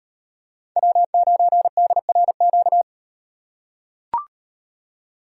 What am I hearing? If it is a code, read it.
Morse code, heard as a keyed tone, about 38 wpm, 700 Hz. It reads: W9DRQ